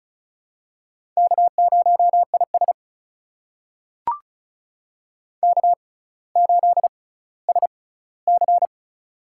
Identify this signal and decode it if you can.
Morse code, heard as a keyed tone, about 36 wpm, spaced as for 18 wpm, 700 Hz. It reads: K0IS K8SC
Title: USA Call Signs - 6x Character Spacing - 35wpm